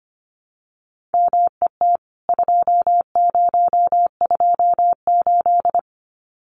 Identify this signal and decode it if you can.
Morse code, heard as a keyed tone, about 25 wpm, 700 Hz. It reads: MET 2028
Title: CWT Exchange 25wpm